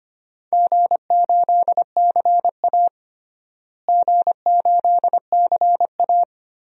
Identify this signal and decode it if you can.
Morse code, heard as a keyed tone, about 25 wpm, 700 Hz. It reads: G8CA G8CA
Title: DXCC-100 Call Signs - 4 Characters - Repeated 2x - 25wpm